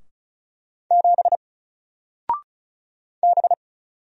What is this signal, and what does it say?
Morse code, heard as a keyed tone, about 34 wpm, 700 Hz. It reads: 7 B
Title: Single Character (Letter:Number:Punctuation:Prosign) 35wpm